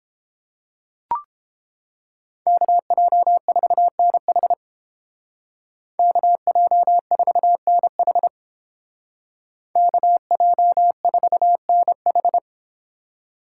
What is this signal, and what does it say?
Morse code, heard as a keyed tone, about 29 wpm, 700 Hz. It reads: KJ4NH KJ4NH KJ4NH